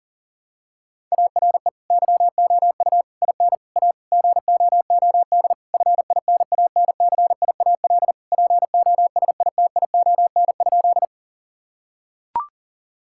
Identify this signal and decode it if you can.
Morse code, heard as a keyed tone, about 40 wpm, 700 Hz. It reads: ARE YOU IN A GOOD FINANCIAL POSITION?
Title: Sentences from Top 1000 Words - Review Part 5 - 40wpm